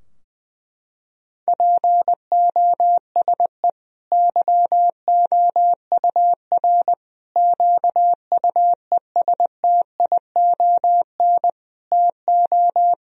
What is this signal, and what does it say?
Morse code, heard as a keyed tone, about 21 wpm, 700 Hz. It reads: POSE YOUR QUESTION TO
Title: Sentences from Top 1000 Words - Review Part 2 - 20wpm